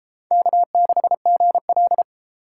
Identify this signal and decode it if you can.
Morse code, heard as a keyed tone, about 32 wpm, 700 Hz. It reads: K6GL